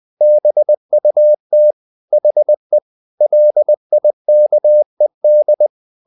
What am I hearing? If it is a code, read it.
Morse code, heard as a keyed tone, about 20 wpm, 600 Hz. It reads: BUT HE LIKED
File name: the old man and the sea part 2 20wpm